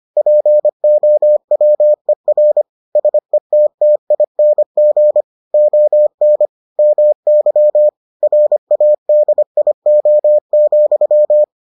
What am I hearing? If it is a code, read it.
Morse code, heard as a keyed tone, about 25 wpm, 600 Hz. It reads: POWER SETTING ON MY RADIO,